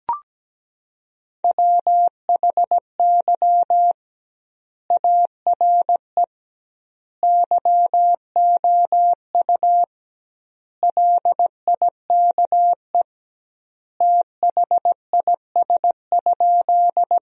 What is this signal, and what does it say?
Morse code, heard as a keyed tone, about 17 wpm, 700 Hz. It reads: WHY ARE YOU LIKE THIS?
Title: 2x Word-Spacing - Sentences from Top 200 Words - 17wpm